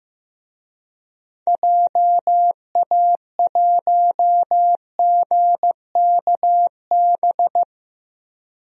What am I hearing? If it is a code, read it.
Morse code, heard as a keyed tone, about 15 wpm, 700 Hz. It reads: JA1GKB
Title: DXCC-100 Call Signs - 6 Characters - Repeated 2x - 15wpm